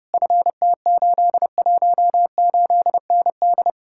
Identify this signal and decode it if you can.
Morse code, heard as a keyed tone, about 33 wpm, 700 Hz. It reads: FT818ND